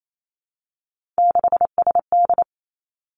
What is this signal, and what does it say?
Morse code, heard as a keyed tone, about 28 wpm, 700 Hz. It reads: 6SD
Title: Three Challenging Letter-Number Combinations 28wpm